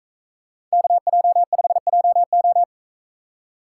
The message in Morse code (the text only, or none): KJ5JO